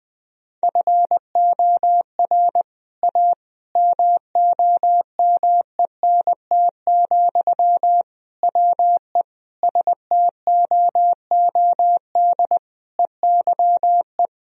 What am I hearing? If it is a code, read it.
Morse code, heard as a keyed tone, about 21 wpm, 700 Hz. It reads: FOR A MOMENT, WE STOOD EYE
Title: Sentences from Top 1000 Words - Review Part 2 - 20wpm